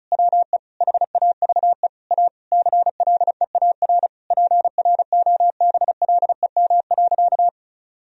Morse code, heard as a keyed tone, about 36 wpm, 700 Hz. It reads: WE HAVE A CLEAR PROBLEM.
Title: Sentences from Top 1000 Words - Review Part 2 - 35wpm